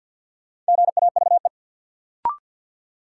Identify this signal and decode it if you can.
Morse code, heard as a keyed tone, about 51 wpm, 700 Hz. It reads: GAVE
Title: Sentences from Top 500 Words - New Word Drill - 50wpm